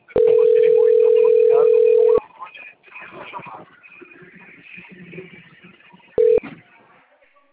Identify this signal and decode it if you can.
Ringback tone (North America), 440 + 480 Hz.
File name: rg-1600-unknown- - - .36580